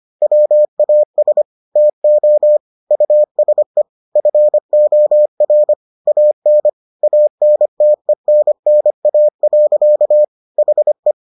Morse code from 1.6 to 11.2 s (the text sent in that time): TO USE FOR AN ANTENNA. HE